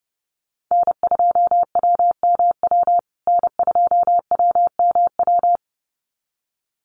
Morse code, heard as a keyed tone, about 30 wpm, 700 Hz. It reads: N2WMW N2WMW
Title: USA Call Signs Repeated 2x 30wpm